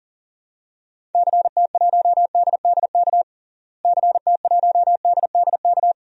Morse code, heard as a keyed tone, about 40 wpm, 700 Hz. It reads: CT1DDK CT1DDK
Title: DXCC-100 Call Signs - 6 Characters - Repeated 3x - 40wpm